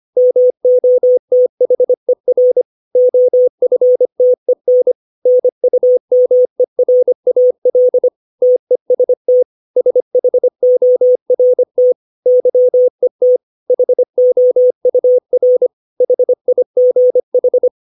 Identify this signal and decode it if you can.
Morse code, heard as a keyed tone, about 25 wpm, 500 Hz. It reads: MOTHER OFTEN NUMERAL TEST SHORT YET HOUR HIGH
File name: 632 words / 632 Words 25WPM